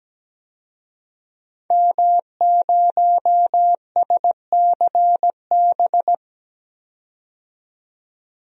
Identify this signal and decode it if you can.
Morse code, heard as a keyed tone, about 17 wpm, 700 Hz. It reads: M0SCB
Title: DXCC-100 Call Signs - 5 Characters - 17wpm